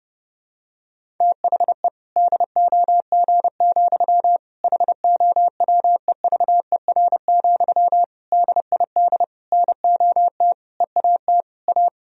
Morse code, heard as a keyed tone, about 30 wpm, 700 Hz. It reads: THE DOG, HOWEVER, DID NOT EAT A
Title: Sentences from Top 1000 Words - Review Part 3 - 30wpm